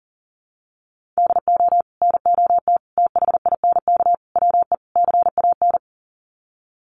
Morse code, heard as a keyed tone, about 40 wpm, 700 Hz. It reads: DO NOT THINK WE CAN
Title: Sets of 5 Words Encore2 40wpm